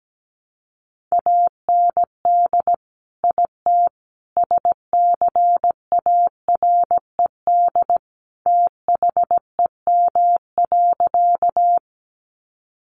Morse code, heard as a keyed tone, about 17 wpm, 700 Hz. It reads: AND IT SCARED THEM.
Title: Sentences from Top 700 Words 17wpm